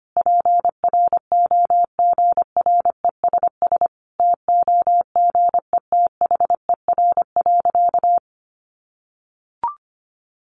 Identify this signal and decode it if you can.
Morse code, heard as a keyed tone, about 25 wpm, 700 Hz. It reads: PROGRESS TOGETHER.
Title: Sentences from Top 1400 Words 25wpm